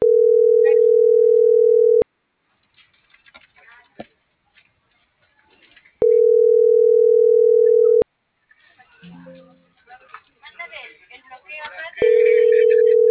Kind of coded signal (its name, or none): telephone call progress tone